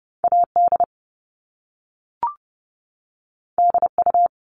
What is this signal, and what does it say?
Morse code, heard as a keyed tone, about 30 wpm, 700 Hz. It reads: AD DU
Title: Top Two Letter Combinations in Words 30wpm